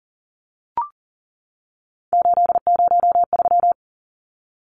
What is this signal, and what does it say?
Morse code, heard as a keyed tone, about 40 wpm, 700 Hz. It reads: 803
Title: Numbers - 3-Digit 40wpm